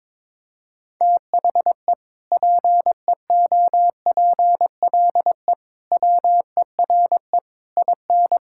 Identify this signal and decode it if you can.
Morse code, heard as a keyed tone, about 22 wpm, 700 Hz. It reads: THE PEOPLE WERE IN
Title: Sentences from Top 1600 Words 22wpm